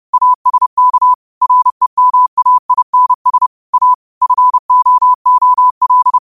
Morse code, heard as a keyed tone, about 30 wpm, 1000 Hz. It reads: ASK REMAINS A FOOL
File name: qotdfiles / 30wpm